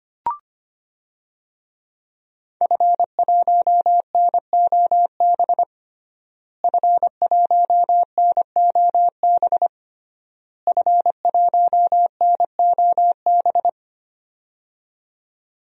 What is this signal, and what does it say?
Morse code, heard as a keyed tone, about 25 wpm, 700 Hz. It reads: F1NOB F1NOB F1NOB
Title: DXCC-100 Call Signs - 5 Characters - Repeated 3x - 25wpm